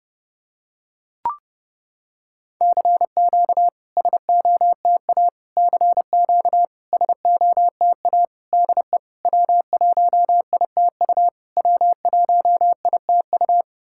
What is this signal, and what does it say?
Morse code, heard as a keyed tone, about 30 wpm, 700 Hz. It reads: CQ SOTA CQ SOTA DE W1ITU W1ITU
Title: SOTA QSO Practice 30wpm